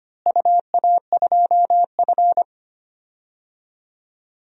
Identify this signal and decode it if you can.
Morse code, heard as a keyed tone, about 25 wpm, 700 Hz. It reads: UA2F
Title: DXCC-100 Call Signs - 4 Characters - Repeated 3x - 25wpm